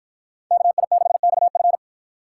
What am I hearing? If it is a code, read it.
Morse code, heard as a keyed tone, about 53 wpm, 700 Hz. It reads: KI6XF